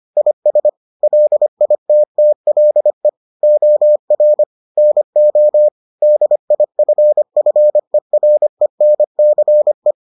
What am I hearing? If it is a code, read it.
Morse code, heard as a keyed tone, about 25 wpm, 600 Hz. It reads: IS LITTLE OR NO DIFFERENCE